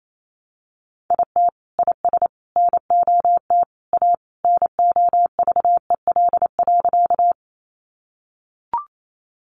Morse code, heard as a keyed tone, about 28 wpm, 700 Hz. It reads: IT IS NOT A NOVEL.
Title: Sentences from Top 1300 Words 28wpm